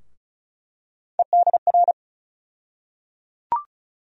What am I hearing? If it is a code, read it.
Morse code, heard as a keyed tone, about 36 wpm, 700 Hz. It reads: EDR
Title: USA Call Sign Postfixes - 3 Letters 35wpm